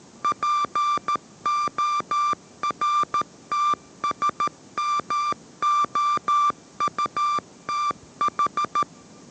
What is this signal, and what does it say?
Morse code, heard as a keyed tone, about 13 wpm, 1180 Hz. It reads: PORTSMOUTH
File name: 14wpm